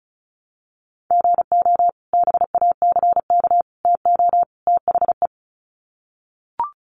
Morse code, heard as a keyed tone, about 34 wpm, 700 Hz. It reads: GO BACK TO THE
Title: Sets of 4 Words 35wpm